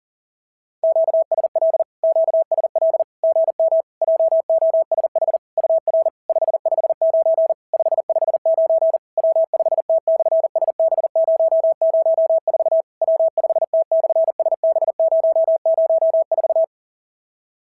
Morse code, heard as a keyed tone, about 40 wpm, 650 Hz. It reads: QSL QSL GM JOSH UR 559 559 W5T/SB004 W5T/SB004
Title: SOTA QSO Practice 40wpm